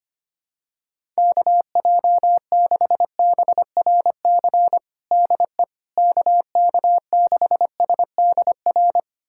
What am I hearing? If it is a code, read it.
Morse code, heard as a keyed tone, about 25 wpm, 700 Hz. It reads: KJ6BRC DE KK6SDR
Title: USA Callsign DE Callsign - 6 Characters - 25wpm